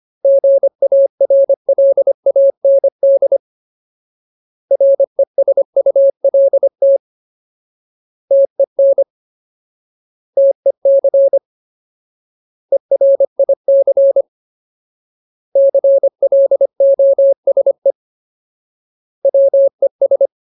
Morse code, heard as a keyed tone, about 25 wpm, 550 Hz. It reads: GARLAND RESULT TEN TEC ERIC CLOSE WES
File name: Ultimate Word List / 25WPM Ultimate